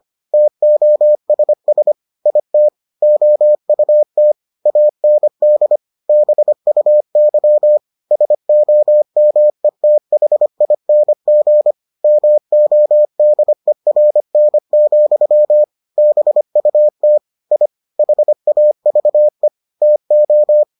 Morse code, heard as a keyed tone, about 25 wpm, 600 Hz. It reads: TOSS IT OUT AND BUY SOMETHING MODERN, BUT I HAVE TO